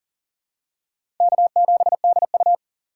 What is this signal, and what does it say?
Morse code, heard as a keyed tone, about 39 wpm, 700 Hz. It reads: K7DU
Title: USA Call Signs - 4 Characters - 4x Character Spacing - 40wpm